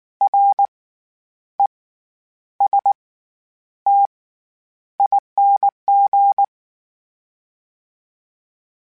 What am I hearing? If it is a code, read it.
Morse code, heard as a keyed tone, about 19 wpm, spaced as for 9 wpm, 800 Hz. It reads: RESTING
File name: Special INGs / Special INGs-mp3s / SlowToFaster ing @18